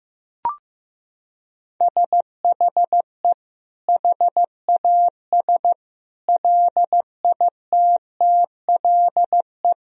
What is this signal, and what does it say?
Morse code, heard as a keyed tone, about 16 wpm, 700 Hz. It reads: SHE HAS LITTLE
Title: Sentences from Top 1000 Words - Review Part 3 - 15wpm